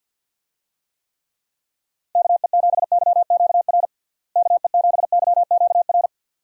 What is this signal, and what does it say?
Morse code, heard as a keyed tone, about 53 wpm, 700 Hz. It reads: KE7YQR KE7YQR
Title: USA Call Signs - 6 Characters - Repeated 2x - 50wpm